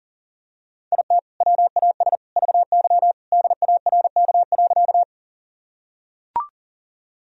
Morse code, heard as a keyed tone, about 40 wpm, 700 Hz. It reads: IT WAS VY DARK.
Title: Sentences from Top 500 Words and Abbreviations 40wpm